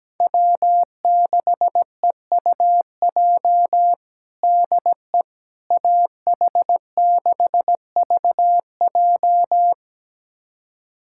Morse code, heard as a keyed tone, about 17 wpm, 700 Hz. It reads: W6EUJ DE AH6VJ
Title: USA Callsign DE Callsign - 5 Characters - 17wpm